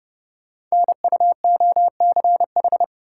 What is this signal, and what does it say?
Morse code, heard as a keyed tone, about 28 wpm, 700 Hz. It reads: NUOCH
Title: Code Groups - 5 Letters - 30wpm